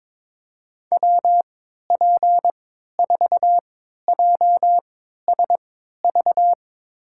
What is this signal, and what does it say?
Morse code, heard as a keyed tone, about 22 wpm, spaced as for 13 wpm, 700 Hz. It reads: WP4JSV
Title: USA Call Signs - 6 Characters - 3x Character Spacing - 22wpm